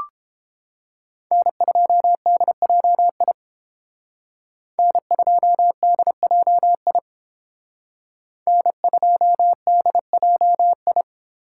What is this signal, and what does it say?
Morse code, heard as a keyed tone, about 29 wpm, 700 Hz. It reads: N2DJI N2DJI N2DJI